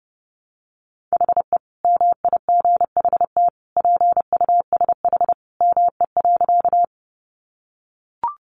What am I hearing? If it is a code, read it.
Morse code, heard as a keyed tone, about 30 wpm, 700 Hz. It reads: HE MIGHT PUSH ME.
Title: Sentences from Top 1300 Words 30wpm